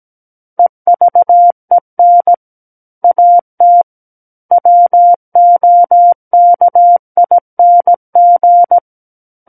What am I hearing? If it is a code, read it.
Morse code, heard as a keyed tone, about 17 wpm, 700 Hz. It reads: EVEN AT WOKING
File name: Book One / 08-Chapter 8 - 17 WPM